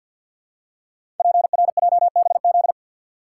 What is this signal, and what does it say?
Morse code, heard as a keyed tone, about 50 wpm, 700 Hz. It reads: PRJBZ
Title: Code Groups - 5 Letters - 50wpm